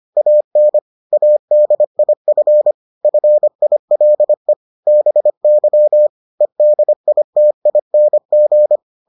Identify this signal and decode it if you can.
Morse code, heard as a keyed tone, about 25 wpm, 600 Hz. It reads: AN ADIF FILE BY EDITING